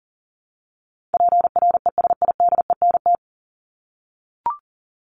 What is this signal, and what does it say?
Morse code, heard as a keyed tone, about 40 wpm, 700 Hz. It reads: PRESIDENT
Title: Top 751-1000 Words 40wpm